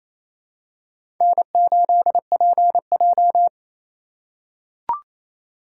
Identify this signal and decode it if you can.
Morse code, heard as a keyed tone, about 28 wpm, 700 Hz. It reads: N8PJ
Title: USA Call Signs - 4 Characters - 3x Character Spacing - 28wpm